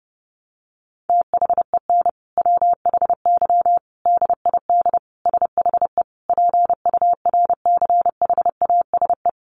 Morse code, heard as a keyed tone, about 30 wpm, 700 Hz. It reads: THEN WHY DID SHE PURCHASE
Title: Sentences from Top 800 Words 30wpm